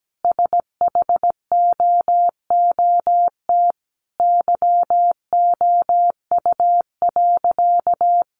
Morse code, heard as a keyed tone, about 17 wpm, 700 Hz. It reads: SHOOT YOU.
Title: Sentences from Top 1200 Words 17wpm